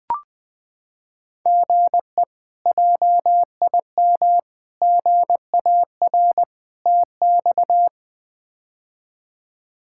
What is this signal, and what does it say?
Morse code, heard as a keyed tone, about 20 wpm, 700 Hz. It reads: GE JIM GAR TX
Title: K1USN SST QSO Practice - 20wpm